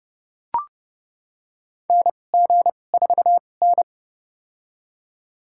Morse code, heard as a keyed tone, about 32 wpm, spaced as for 22 wpm, 700 Hz. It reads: NG4N
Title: USA Call Signs - 2x Character Spacing - 30wpm